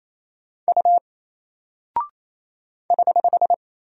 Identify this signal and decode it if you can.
Morse code, heard as a keyed tone, about 28 wpm, 700 Hz. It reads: U <HH>
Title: Single Character (Letter:Number:Punctuation:Prosign) Alt 28wpm